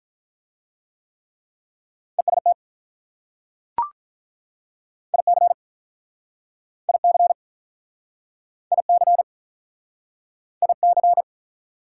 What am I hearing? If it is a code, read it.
Morse code, heard as a keyed tone, about 43 wpm, 690 Hz. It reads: EST IC IC IC IC
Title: Speed-Racing Prefixes and Suffixes 35wpm